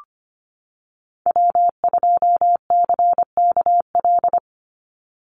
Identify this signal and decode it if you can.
Morse code, heard as a keyed tone, about 25 wpm, 700 Hz. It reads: W2CKL
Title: NA SOTA Activator Callsigns 25wpm